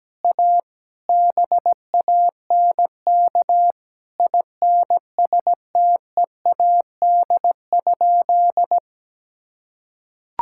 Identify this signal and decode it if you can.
Morse code, heard as a keyed tone, about 17 wpm, 700 Hz. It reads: A BANK INSTEAD?
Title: Sentences from Top 1000 Words - Review Part 6 - 17wpm